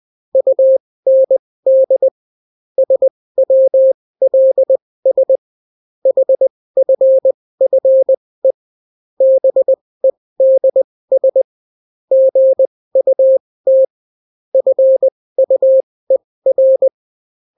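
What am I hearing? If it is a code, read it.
Morse code, heard as a keyed tone, about 20 wpm, spaced as for 16 wpm, 530 Hz. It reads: UND SWLS HFFE BEDS GUT FUER
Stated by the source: Daily Morse weather and top stories at 16 wpm, clean without QRN.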